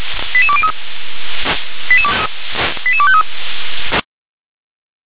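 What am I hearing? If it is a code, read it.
Five-tone selective call (ZVEI), heard as quick runs of five tones, tones of about 70 ms. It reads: 88203 88253
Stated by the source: ZVEI five-tone selective call received by SDR.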